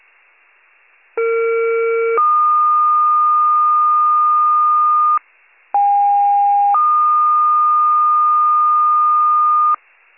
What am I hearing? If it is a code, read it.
Two-tone page, heard as a one-second tone, then a three-second tone, about 460 then 1220 Hz, about 810 then 1220 Hz. It is heard from 1.2 to 9.8 s.